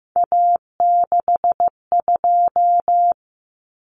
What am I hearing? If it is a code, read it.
Morse code, heard as a keyed tone, about 15 wpm, 700 Hz. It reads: A62
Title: Common Cut Numbers - 3-Digit 15wpm